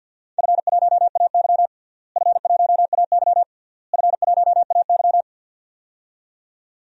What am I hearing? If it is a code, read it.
Morse code, heard as a keyed tone, about 47 wpm, 700 Hz. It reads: F1AY F1AY F1AY
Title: DXCC-100 Call Signs - 4 Characters - Repeated 3x - 50wpm